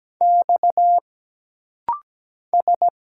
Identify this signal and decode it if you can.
Morse code, heard as a keyed tone, about 17 wpm, 700 Hz. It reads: X S
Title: Single Letters - Rapid-Fire 17wpm